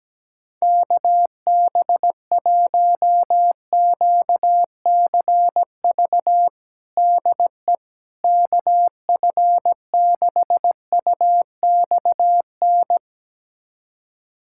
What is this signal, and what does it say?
Morse code, heard as a keyed tone, about 17 wpm, 700 Hz. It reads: KB1QCV DE KF6UXN
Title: USA Callsign DE Callsign - 6 Characters - 17wpm